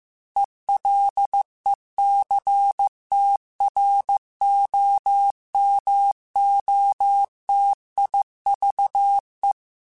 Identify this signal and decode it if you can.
Morse code, heard as a keyed tone, about 15 wpm, 800 Hz. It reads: ELECTROMOTIVE
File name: Technical Subjects / Technical Subjects-mp3s / XL @13